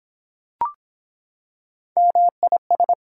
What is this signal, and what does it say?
Morse code, heard as a keyed tone, about 23 wpm, 700 Hz. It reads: MIS